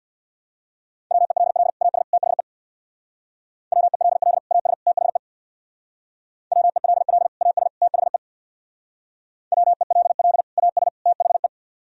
Morse code, heard as a keyed tone, about 63 wpm, 700 Hz. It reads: WELL AS THE WELL AS THE WELL AS THE WELL AS THE